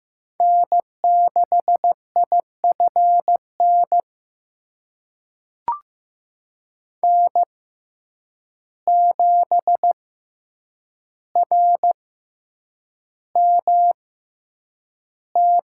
Morse code, heard as a keyed tone, about 15 wpm, spaced as for 8 wpm, 700 Hz. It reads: N6IFN N7RMT
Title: USA Call Signs - 5 Characters - 6x Character Spacing - 15wpm